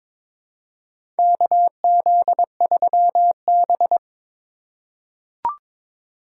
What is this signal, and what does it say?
Morse code, heard as a keyed tone, about 22 wpm, 700 Hz. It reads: KZ3B
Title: USA Call Signs - 4 Characters - 4x Character Spacing - 22wpm